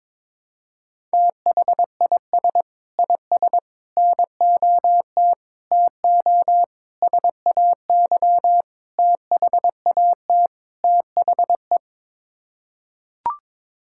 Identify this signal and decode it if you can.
Morse code, heard as a keyed tone, about 22 wpm, 700 Hz. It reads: THIS IS NOT TO SAY THAT THE
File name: practice / 22wpm / Sets of 7 Words - 22wpm